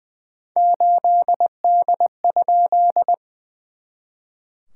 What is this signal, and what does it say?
Morse code, heard as a keyed tone, about 20 wpm, 700 Hz. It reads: 8D?
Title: SOTA QSO Practice 20wpm